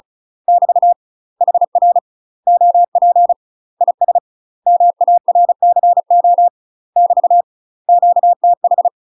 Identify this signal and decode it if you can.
Morse code, heard as a keyed tone, about 36 wpm, 700 Hz. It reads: <BT> HR OP IS MARCO <BT> QTH
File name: UK MASTER QSOs TRAINING / 35WPM / UK MASTER QSOs 0026